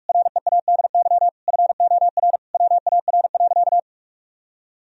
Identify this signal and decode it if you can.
Morse code, heard as a keyed tone, about 46 wpm, 700 Hz. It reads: READY FOR WAR.
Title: Sentences from Top 1000 Words - Review Part 3 - 45wpm